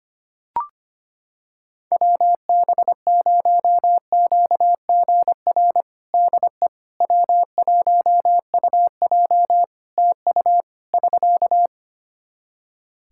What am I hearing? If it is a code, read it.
Morse code, heard as a keyed tone, about 25 wpm, 700 Hz. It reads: WB0QGR DE W1UJ TU <SK>
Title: Practice QSO - Exchange 4 - Ending - 25wpm